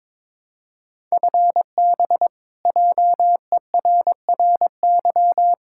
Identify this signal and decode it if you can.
Morse code, heard as a keyed tone, about 22 wpm, 700 Hz. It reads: FB JERRY
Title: K1USN SST QSO Practice 22wpm